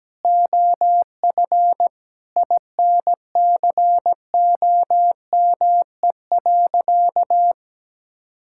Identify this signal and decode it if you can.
Morse code, heard as a keyed tone, about 17 wpm, 700 Hz. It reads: OF INCOME.
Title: Sentences from Top 1000 Words - Review Part 5 - 17wpm